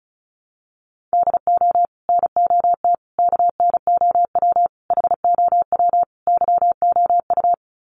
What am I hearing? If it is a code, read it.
Morse code, heard as a keyed tone, about 34 wpm, 700 Hz. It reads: DO NOT KNOW HOW YOU
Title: Sets of 5 Words Encore 35wpm